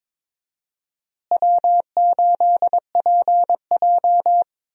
Morse code, heard as a keyed tone, about 22 wpm, 700 Hz. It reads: W8PJ
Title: USA Call Signs - 4 Characters - Repeated 3x - 22wpm